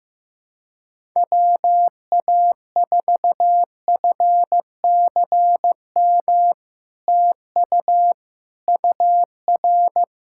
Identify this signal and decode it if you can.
Morse code, heard as a keyed tone, about 15 wpm, 700 Hz. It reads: WA4FCM TU UR
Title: SOTA QSO Practice 15wpm